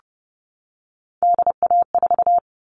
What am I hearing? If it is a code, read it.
Morse code, heard as a keyed tone, about 30 wpm, 700 Hz. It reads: DA4
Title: All Cut Numbers - 3-Digit 30wpm